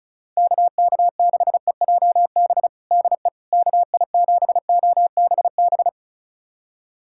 Morse code, heard as a keyed tone, about 34 wpm, 700 Hz. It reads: KK6EJB DE KI7OBB
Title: USA Callsign DE Callsign - 6 Characters - 35wpm